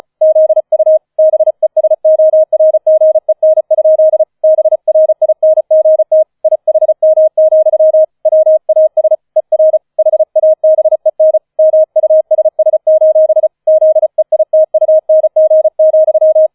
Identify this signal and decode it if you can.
Morse code, heard as a keyed tone, about 36 wpm, 630 Hz. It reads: ZU BESORGEN? BRINGT IHM, WAS ER HABEN MUSS: ZEITUNG,
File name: 005 Fünfter Streich 35wpm